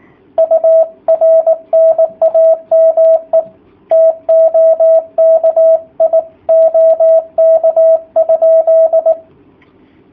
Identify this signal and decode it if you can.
Morse code, heard as a keyed tone, about 18 wpm, 650 Hz, with radio noise. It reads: UR NAME TOKI OK?
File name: jh8cbh / 2019cw / sp18- w